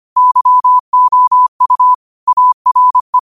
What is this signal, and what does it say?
Morse code, heard as a keyed tone, about 25 wpm, 1000 Hz. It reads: YOU ARE
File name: qotdfiles / 25wpm